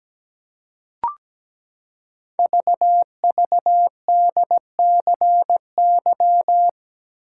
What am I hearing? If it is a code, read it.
Morse code, heard as a keyed tone, about 17 wpm, 700 Hz. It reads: VVDCY
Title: Code Groups - 5 Letters - 17wpm